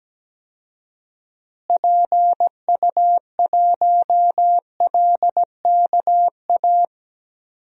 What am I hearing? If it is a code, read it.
Morse code, heard as a keyed tone, about 17 wpm, 700 Hz. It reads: PU1LKA
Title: DXCC-100 Call Signs - 6 Characters - Repeated 3x - 17wpm